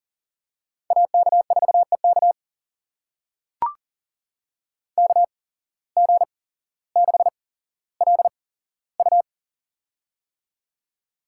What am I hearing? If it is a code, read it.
Morse code, heard as a keyed tone, about 38 wpm, spaced as for 17 wpm, 700 Hz. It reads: AK4EK KG6LU
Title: USA Call Signs - 5 Characters - 8x Character Spacing - 40wpm